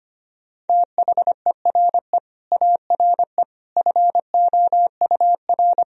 Morse code, heard as a keyed tone, about 25 wpm, 700 Hz. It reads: THERE ARE FOUR
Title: Sentences from Top 1000 Words - Review Part 1 - 25wpm